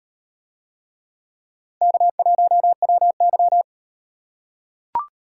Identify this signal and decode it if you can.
Morse code, heard as a keyed tone, about 38 wpm, 700 Hz. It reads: K1WY